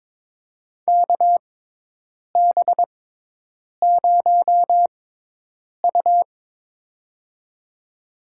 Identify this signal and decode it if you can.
Morse code, heard as a keyed tone, about 22 wpm, spaced as for 8 wpm, 700 Hz. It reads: KB0U
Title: USA Call Signs - 4 Characters - 6x Character Spacing - 22wpm